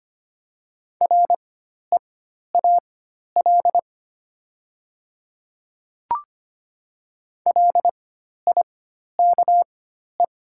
Morse code, heard as a keyed tone, about 25 wpm, spaced as for 12 wpm, 700 Hz. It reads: REAL LIKE
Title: 4x Character Spacing - Top 300 Words - 25wpm